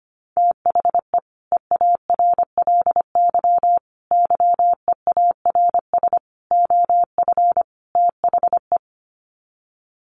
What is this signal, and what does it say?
Morse code, heard as a keyed tone, about 25 wpm, 700 Hz. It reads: THE EARLY YEARS OF THE
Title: Sets of 5 Words Encore 25wpm